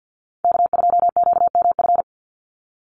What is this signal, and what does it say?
Morse code, heard as a keyed tone, about 50 wpm, 700 Hz. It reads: K2QMF
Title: Call Signs Repeated 2x - 50wpm